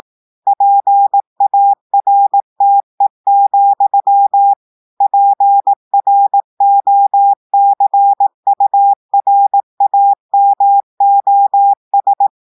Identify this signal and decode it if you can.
Morse code, heard as a keyed tone, about 18 wpm, 800 Hz. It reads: PARTE, PROCURAMOS
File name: audios / portugais / portugais texte en clair 1 vitesse 18 mots minutes